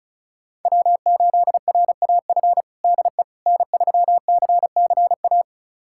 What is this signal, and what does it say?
Morse code, heard as a keyed tone, about 36 wpm, 700 Hz. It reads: W8RAF DE N3CCA
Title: USA Callsign DE Callsign - 5 Characters - 35wpm